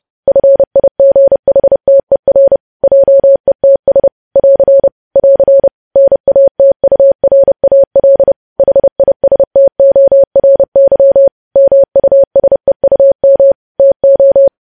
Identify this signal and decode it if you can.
Morse code, heard as a keyed tone, about 30 wpm, 550 Hz. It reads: FIGHTER JETS <AR> <AR> NATURAL HISTORY MUSEUM TO
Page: News Headlines in Morse Code at 30 WPM